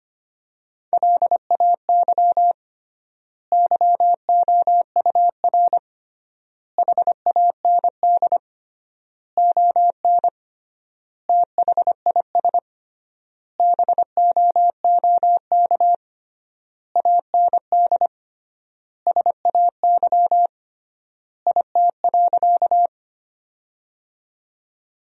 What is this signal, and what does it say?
Morse code, heard as a keyed tone, about 25 wpm, 700 Hz. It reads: LAY YOUR HAND ON THIS BOOK AND SAY IT.
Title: 3x Word-Spacing - Sentences from Top 400 Words - 25wpm